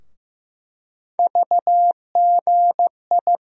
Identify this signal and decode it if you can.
Morse code, heard as a keyed tone, about 15 wpm, 700 Hz. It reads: VGI
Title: USA Call Sign Postfixes - 3 Letters 15wpm